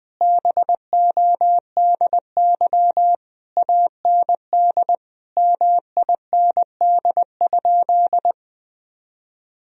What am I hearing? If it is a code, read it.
Morse code, heard as a keyed tone, about 20 wpm, 700 Hz. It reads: BODY AND MIND?
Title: Sentences from Top 1000 Words - Review Part 2 - 20wpm